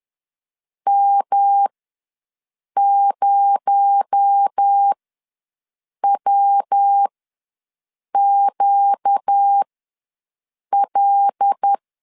Morse code, heard as a keyed tone, about 10 wpm, spaced as for 6 wpm, 800 Hz. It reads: M0WQL
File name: LEZIONI / 6 PKX0 / 6 PKX0 ws 10wpm 4220